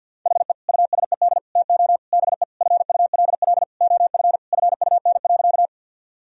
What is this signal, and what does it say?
Morse code, heard as a keyed tone, about 50 wpm, 700 Hz. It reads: HE USED TO BE FULL OF FUN.
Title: Sentences from Top 1000 Words - Review Part 4 - 50wpm